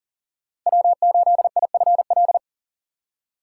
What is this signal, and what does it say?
Morse code, heard as a keyed tone, about 49 wpm, 700 Hz. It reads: W8IFL
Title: Speed Racing - USA 1x3 Call Signs 40wpm